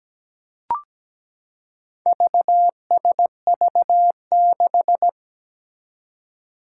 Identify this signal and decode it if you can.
Morse code, heard as a keyed tone, about 17 wpm, 700 Hz. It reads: VSV6
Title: Four Challenging Letter-Number Combinations 17wpm